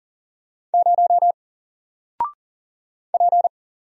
Morse code, heard as a keyed tone, about 40 wpm, 700 Hz. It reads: 0 P
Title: Single Character (Letter:Number:Punctuation:Prosign) Alt 40wpm